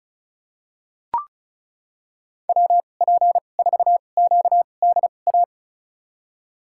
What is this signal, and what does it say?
Morse code, heard as a keyed tone, about 34 wpm, spaced as for 25 wpm, 700 Hz. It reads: WP4QDA
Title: USA Call Signs - 6 Characters - 2x Character Spacing - 35wpm